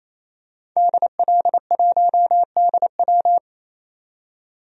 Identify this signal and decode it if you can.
Morse code, heard as a keyed tone, about 28 wpm, 700 Hz. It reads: DL1DW
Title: DXCC-100 Call Signs - 5 Characters - 28wpm